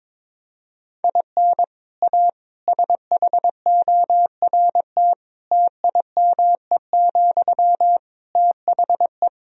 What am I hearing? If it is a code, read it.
Morse code, heard as a keyed tone, about 22 wpm, 700 Hz. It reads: IN A SHORT TIME, THE
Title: Sentences from Top 1000 Words - Review Part 2 - 22wpm